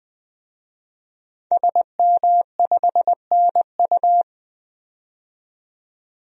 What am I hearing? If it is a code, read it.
Morse code, heard as a keyed tone, about 20 wpm, 700 Hz. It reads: SM5NU
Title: DXCC-100 Call Signs - 5 Characters - 20wpm